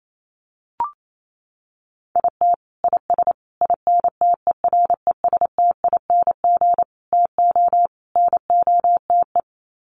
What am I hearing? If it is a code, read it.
Morse code, heard as a keyed tone, about 28 wpm, 700 Hz. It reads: IT IS INTERESTING TO NOTE
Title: Sets of 5 Words Encore2 28wpm